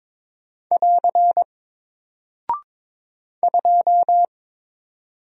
Morse code, heard as a keyed tone, about 22 wpm, 700 Hz. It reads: <AR> 2
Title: Single Character (Letter:Number:Punctuation:Prosign) Alt 22wpm